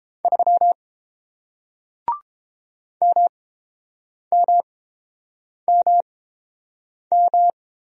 Morse code, heard as a keyed tone, about 29 wpm, spaced as for 7 wpm, 700 Hz. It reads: 3 MMMM